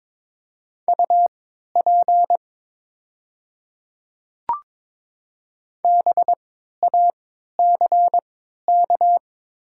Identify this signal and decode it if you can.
Morse code, heard as a keyed tone, about 22 wpm, spaced as for 12 wpm, 690 Hz. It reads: UP BACK
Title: 3x Character Spacing - Top 200 Words - 22wpm